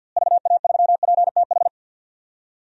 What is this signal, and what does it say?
Morse code, heard as a keyed tone, about 51 wpm, 700 Hz. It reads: VA3PTH
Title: DXCC-100 Call Signs - 6 Characters - Repeated 2x - 50wpm